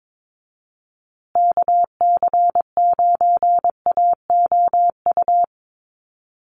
Morse code, heard as a keyed tone, about 22 wpm, 700 Hz. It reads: KC9AOU
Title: USA Call Signs Repeated 3x 22wpm